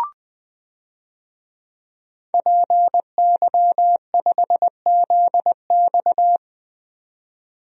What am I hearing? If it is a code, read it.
Morse code, heard as a keyed tone, about 20 wpm, 700 Hz. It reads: PY5ZX
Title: DXCC-100 Call Signs - 5 Characters - 20wpm